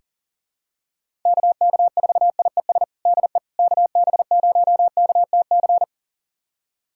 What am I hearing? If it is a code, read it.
Morse code, heard as a keyed tone, about 40 wpm, 700 Hz. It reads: KK4IES DE KB0KTC
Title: USA Callsign DE Callsign 40wpm